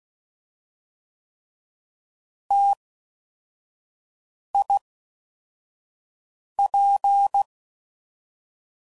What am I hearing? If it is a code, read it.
Morse code, heard as a keyed tone, about 16 wpm, 800 Hz. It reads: T I P
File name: Mixed / Variable Sp / T AOEIs